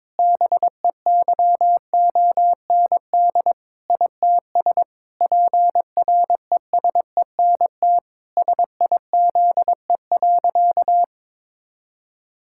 Morse code, heard as a keyed tone, about 22 wpm, 700 Hz. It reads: BEYOND ITS PRESENT SIZE.
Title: Sentences from Top 1000 Words - Review Part 6 - 22wpm